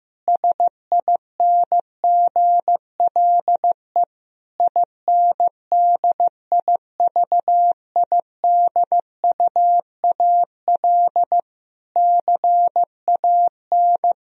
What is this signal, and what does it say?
Morse code, heard as a keyed tone, about 15 wpm, 700 Hz. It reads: SINGLE INDIVIDUAL CAN
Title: Sentences from Top 1000 Words - Review Part 4 - 15wpm